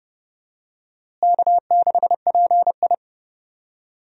Morse code, heard as a keyed tone, about 30 wpm, 700 Hz. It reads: K6PI
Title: USA Call Signs - 4 Characters - Repeated 2x - 30wpm